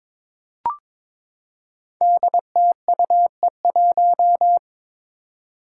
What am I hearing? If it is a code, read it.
Morse code, heard as a keyed tone, about 22 wpm, 700 Hz. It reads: DTUE1
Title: Five Challenging Letter-Number Combinations 22wpm